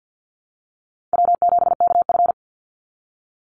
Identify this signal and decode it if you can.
Morse code, heard as a keyed tone, about 50 wpm, 700 Hz. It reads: W7KF
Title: Easy Call Signs - 50wpm